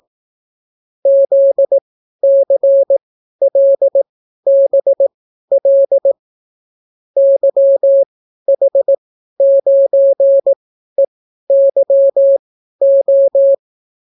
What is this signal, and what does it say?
Morse code, heard as a keyed tone, about 18 wpm, spaced as for 12 wpm, 550 Hz. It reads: ZCLBL YH9EYO